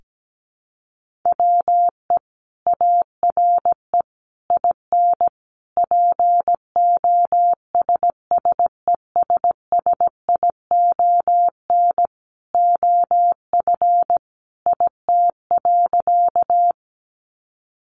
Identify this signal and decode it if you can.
Morse code, heard as a keyed tone, about 17 wpm, 700 Hz. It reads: WE ARE IN POSSESSION OF IT.
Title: Sentences from Top 1400 Words 17wpm